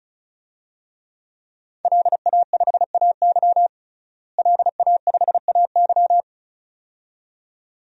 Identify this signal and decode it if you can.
Morse code, heard as a keyed tone, about 34 wpm, 700 Hz. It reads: LA5AY LA5AY
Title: DXCC-100 Call Signs - 5 Characters - Repeated 2x - 35wpm